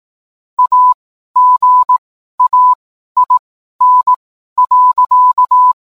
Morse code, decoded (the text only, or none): AGAIN.